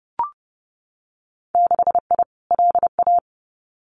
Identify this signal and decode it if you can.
Morse code, heard as a keyed tone, about 30 wpm, 700 Hz. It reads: 6I LA
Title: Winter Field Day Exchanges Encore 30wpm